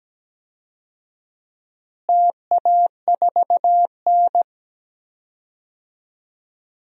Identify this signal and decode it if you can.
Morse code, heard as a keyed tone, about 17 wpm, 700 Hz. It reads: TA4N
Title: DXCC-100 Call Signs - 4 Characters - Repeated 3x - 17wpm